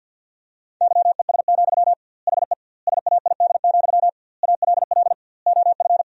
Morse code, heard as a keyed tone, about 49 wpm, 700 Hz. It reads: YES, HE SAID, ALL OF
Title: 2x Word-Spacing - Sentences from Top 500 Words - 50wpm